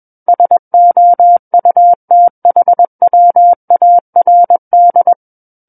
Morse code, heard as a keyed tone, about 21 wpm, 700 Hz. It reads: SOUTHWARD
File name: Book One / 12-Chapter 12 - 21 WPM